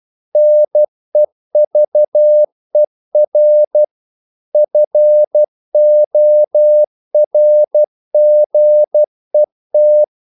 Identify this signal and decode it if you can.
Morse code, heard as a keyed tone, about 12 wpm, 600 Hz. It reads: NEVER FORGET
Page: English quotations in CW (telegraphy - morse)